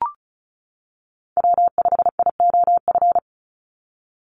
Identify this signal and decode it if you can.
Morse code, heard as a keyed tone, about 36 wpm, 700 Hz. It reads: W5IOF
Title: USA Call Signs 35wpm